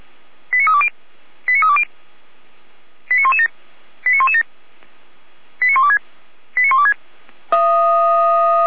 Five-tone selective call (ZVEI), heard as quick runs of five tones, tones of about 70 ms. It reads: 89320 89320 89117 89117 89126 89126